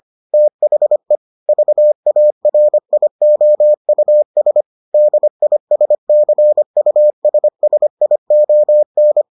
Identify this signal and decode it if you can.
Morse code, heard as a keyed tone, about 25 wpm, 600 Hz. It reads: THE VARIOUS DISCUSSION